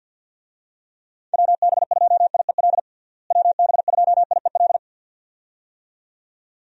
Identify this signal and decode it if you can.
Morse code, heard as a keyed tone, about 48 wpm, 700 Hz. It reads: WB2IEL WB2IEL
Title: USA Call Signs - 6 Characters - Repeated 2x - 50wpm